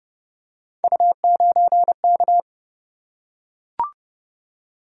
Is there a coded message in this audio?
Yes: Morse code from 0.8 to 2.4 s, heard as a keyed tone, about 30 wpm, 700 Hz. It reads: U9K